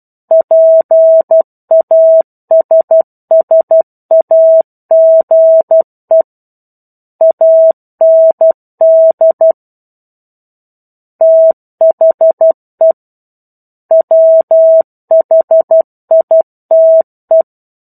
Morse code, heard as a keyed tone, about 12 wpm, 650 Hz. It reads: PASSAGE AND THE WHITE
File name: 01-Chapter-01 12 WPM